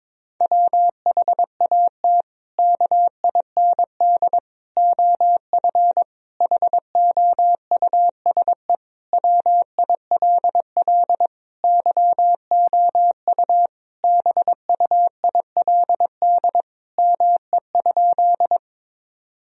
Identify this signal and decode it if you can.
Morse code, heard as a keyed tone, about 22 wpm, 700 Hz. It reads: WHAT KIND OF HOUSE WILL YOU BUILD ME?
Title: Sentences from Top 1000 Words - Review Part 1 - 22wpm